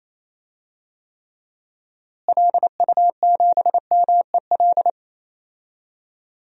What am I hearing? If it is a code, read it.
Morse code, heard as a keyed tone, about 28 wpm, 700 Hz. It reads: LU7MEL
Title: DXCC-100 Call Signs - 6 Characters - Repeated 3x - 28wpm